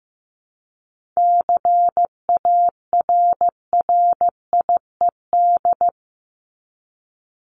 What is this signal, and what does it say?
Morse code, heard as a keyed tone, about 15 wpm, 700 Hz. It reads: CARRIED
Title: ED Suffix - Top 100 Words with Base Words 15wpm